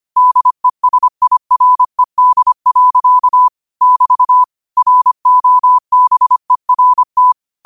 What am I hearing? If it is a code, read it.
Morse code, heard as a keyed tone, about 25 wpm, 1000 Hz. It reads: DESIRED. <BT> ROBERT
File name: qotdfiles / 25wpm